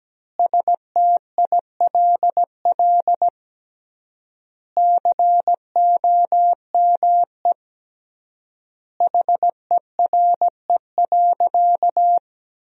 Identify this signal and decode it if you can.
Morse code, heard as a keyed tone, about 17 wpm, 700 Hz. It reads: STILL COME HERE.
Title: 3x Word-Spacing - Sentences from Top 300 Words - 17wpm